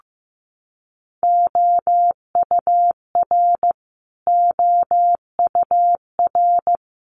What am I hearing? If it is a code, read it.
Morse code, heard as a keyed tone, about 15 wpm, 700 Hz. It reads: OUR OUR
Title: Top 200 Words Repeated 15wpm